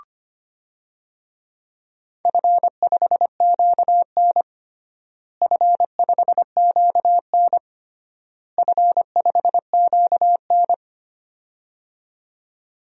Morse code, heard as a keyed tone, about 25 wpm, 700 Hz. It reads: F5QN F5QN F5QN
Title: DXCC-100 Call Signs - 4 Characters - Repeated 3x - 25wpm